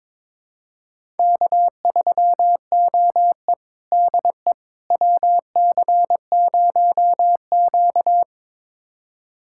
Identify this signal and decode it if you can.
Morse code, heard as a keyed tone, about 21 wpm, 700 Hz. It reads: K3OE DE WC0Q
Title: USA Callsign DE Callsign - 4 Characters - 22wpm